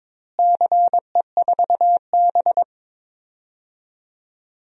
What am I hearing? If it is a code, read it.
Morse code, heard as a keyed tone, about 22 wpm, 700 Hz. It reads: CE4B
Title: DXCC-100 Call Signs - 4 Characters - Repeated 2x - 22wpm